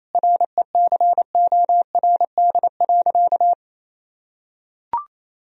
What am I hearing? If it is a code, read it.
Morse code, heard as a keyed tone, about 28 wpm, 700 Hz. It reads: RECORD.
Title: Sentences from Top 1000 Words - Review Part 3 - 28wpm